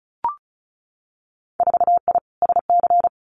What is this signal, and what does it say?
Morse code, heard as a keyed tone, about 36 wpm, 700 Hz. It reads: 4I SC
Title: Winter Field Day Exchanges 35wpm